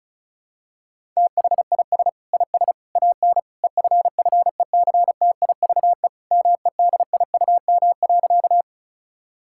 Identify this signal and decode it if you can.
Morse code, heard as a keyed tone, about 36 wpm, 700 Hz. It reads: THIS IS AN EFFECTIVE MEDIUM.
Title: Sentences from Top 1000 Words - Review Part 6 - 35wpm